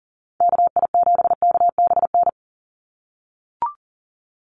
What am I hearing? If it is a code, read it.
Morse code, heard as a keyed tone, about 40 wpm, 700 Hz. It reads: KI7KBN
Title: USA Call Signs 40wpm